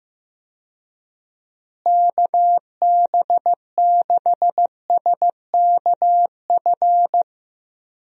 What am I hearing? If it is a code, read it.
Morse code, heard as a keyed tone, about 16 wpm, 700 Hz. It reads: KB6SKF
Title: USA Call Signs - 6 Characters - Repeated 2x - 15wpm